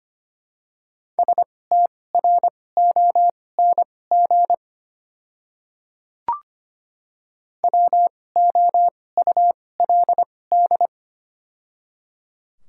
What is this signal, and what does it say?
Morse code, heard as a keyed tone, about 25 wpm, spaced as for 18 wpm, 700 Hz. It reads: STRONG WOULD
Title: 2x Character Spacing - Top 500 Words - 25wpm